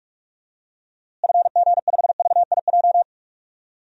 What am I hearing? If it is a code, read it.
Morse code, heard as a keyed tone, about 46 wpm, 700 Hz. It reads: FG5VIJ
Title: DXCC-100 Call Signs - 6 Characters - Repeated 2x - 45wpm